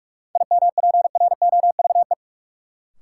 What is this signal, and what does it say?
Morse code, heard as a keyed tone, about 44 wpm, 700 Hz. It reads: IMPROVE
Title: Sentences from Top 800 Words - New Word Drill - 45wpm